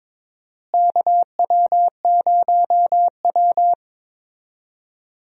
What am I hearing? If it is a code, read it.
Morse code, heard as a keyed tone, about 22 wpm, 700 Hz. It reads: KW0W
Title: USA Call Signs - 4 Characters - Repeated 2x - 22wpm